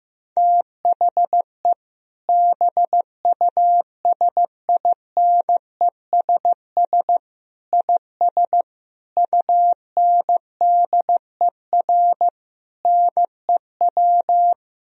Morse code, heard as a keyed tone, about 14 wpm, 700 Hz. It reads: THE BUSINESS IS UNDER NEW
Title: Sentences from Top 1000 Words - Review Part 4 - 15wpm